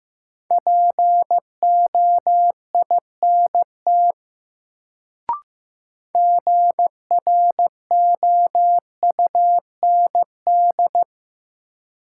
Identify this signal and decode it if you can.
Morse code, heard as a keyed tone, about 15 wpm, 700 Hz. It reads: POINT GROUND
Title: Sets of 3 Words - New Word Drill - 15wpm